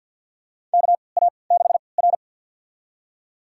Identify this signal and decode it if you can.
Morse code, heard as a keyed tone, about 47 wpm, spaced as for 28 wpm, 700 Hz. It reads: KA6R
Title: USA Call Signs - 4 Characters - 3x Character Spacing - 50wpm